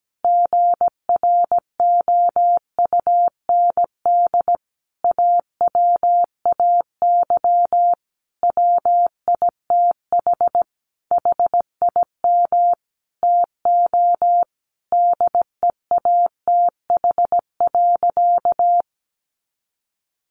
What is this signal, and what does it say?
Morse code, heard as a keyed tone, about 17 wpm, 700 Hz. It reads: GROUND AWAY WITH HIM TO DEATH.
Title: Sentences from Top 1100 Words 17wpm